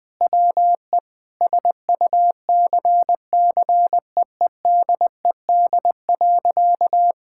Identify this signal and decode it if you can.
Morse code, heard as a keyed tone, about 20 wpm, 700 Hz. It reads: WE SUCCEEDED.
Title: Sentences with 6-Word Phrases - 20wpm